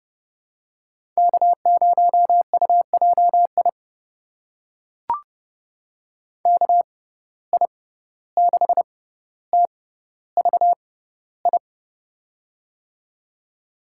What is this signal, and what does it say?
Morse code, heard as a keyed tone, about 30 wpm, spaced as for 15 wpm, 700 Hz. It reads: K0UJI KI6TVI
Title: USA Call Signs - 6x Character Spacing - 30wpm